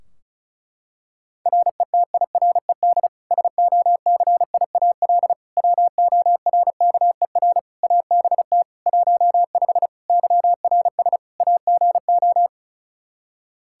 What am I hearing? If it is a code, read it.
Morse code, heard as a keyed tone, about 34 wpm, 700 Hz. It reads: RETIRED SOCIAL WORKER ABT 15 YRS AGO
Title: K1USN SST QSO Practice 35wpm